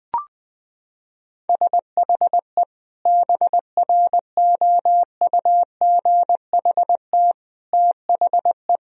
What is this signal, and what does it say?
Morse code, heard as a keyed tone, about 21 wpm, 700 Hz. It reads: SHE BROUGHT THE
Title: Sentences from Top 1000 Words - Review Part 4 - 20wpm